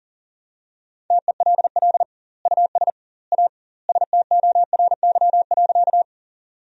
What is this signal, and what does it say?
Morse code, heard as a keyed tone, about 40 wpm, 700 Hz. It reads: TELL US A STORY.
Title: 2x Word-Spacing - Sentences from Top 300 Words - 40wpm